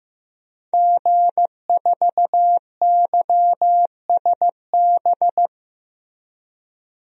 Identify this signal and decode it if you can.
Morse code, heard as a keyed tone, about 16 wpm, 700 Hz. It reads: G4YSB
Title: DXCC-100 Call Signs - 5 Characters - Repeated 3x - 15wpm